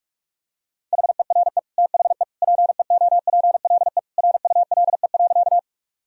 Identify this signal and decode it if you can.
Morse code, heard as a keyed tone, about 46 wpm, 700 Hz. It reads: HERE THE PEOPLE RULE.
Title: Sentences from Top 1000 Words - Review Part 2 - 45wpm